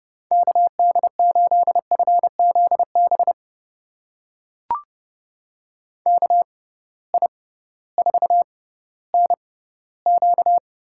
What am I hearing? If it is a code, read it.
Morse code, heard as a keyed tone, about 30 wpm, spaced as for 16 wpm, 700 Hz. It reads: KD8FZB KI4NQ
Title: USA Call Signs - 6 Characters - 6x Character Spacing - 30wpm